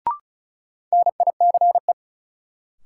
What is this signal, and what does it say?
Morse code, heard as a keyed tone, about 36 wpm, 700 Hz. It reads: NICE
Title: Sentences from Top 1100 Words - New Word Drill Alt - 35wpm